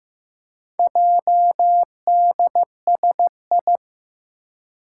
Morse code, heard as a keyed tone, about 15 wpm, 700 Hz. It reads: JDSI
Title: Four Challenging Letter-Number Combinations 15wpm